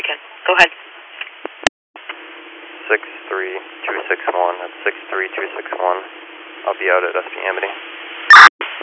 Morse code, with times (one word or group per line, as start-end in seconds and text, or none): none